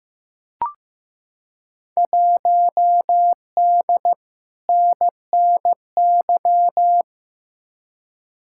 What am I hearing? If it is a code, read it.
Morse code, heard as a keyed tone, about 15 wpm, 700 Hz. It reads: 1D NNY
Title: Field Day Exchanges - 15wpm